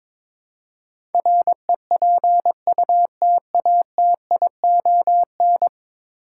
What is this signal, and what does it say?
Morse code, heard as a keyed tone, about 22 wpm, 700 Hz. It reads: REPUTATION
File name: practice / 22wpm / 10 Letter Words - First 3 Letters Given - 22wpm